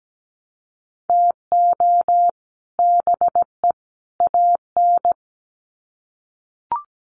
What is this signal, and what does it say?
Morse code, heard as a keyed tone, about 17 wpm, 700 Hz. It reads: TO BE AN
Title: Sets of 3 Words 17wpm